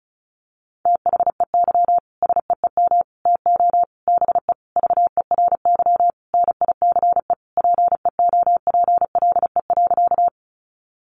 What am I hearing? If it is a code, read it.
Morse code, heard as a keyed tone, about 34 wpm, 700 Hz. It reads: THEY SEEM TO BE VERY NICE PEOPLE.
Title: Sentences from Top 1100 Words 35wpm